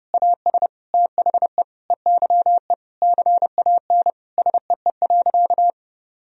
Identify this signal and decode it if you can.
Morse code, heard as a keyed tone, about 30 wpm, 700 Hz. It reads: AS THE EYE CAN SEE.
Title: Sentences from Top 1000 Words - Review Part 1 - 30wpm